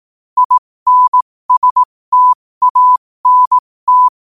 Morse code, heard as a keyed tone, about 18 wpm, spaced as for 15 wpm, 1000 Hz. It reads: INSTANT
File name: qotdfiles / 15wpm